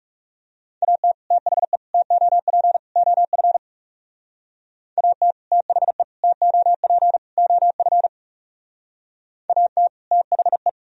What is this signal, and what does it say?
Morse code, heard as a keyed tone, about 38 wpm, 700 Hz. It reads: AT THE TOP OF AT THE TOP OF AT THE